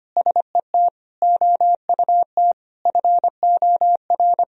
Morse code, heard as a keyed tone, about 25 wpm, 700 Hz. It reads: SET OUT FOR
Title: Sentences from Top 1000 Words - Review Part 1 - 25wpm